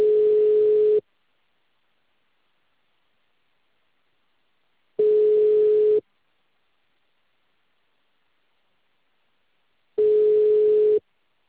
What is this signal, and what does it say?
Ringback tone (Europe, 425 Hz).
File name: out- -2044- - - .73381